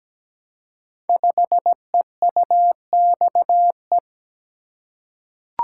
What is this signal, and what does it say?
Morse code, heard as a keyed tone, about 17 wpm, 700 Hz. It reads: 5EUXE
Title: Code Groups - 5 Letter:Numbers - 17wpm